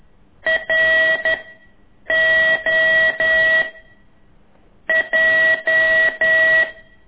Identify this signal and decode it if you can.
Morse code, heard as a keyed tone, about 11 wpm, 1950 Hz, with noise. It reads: R O J
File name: ik1weg2 / CORSI CW